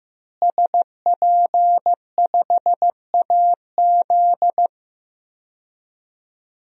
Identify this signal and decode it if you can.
Morse code, heard as a keyed tone, about 15 wpm, 700 Hz. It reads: SP5AZ
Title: DXCC-100 Call Signs - 5 Characters - Repeated 3x - 15wpm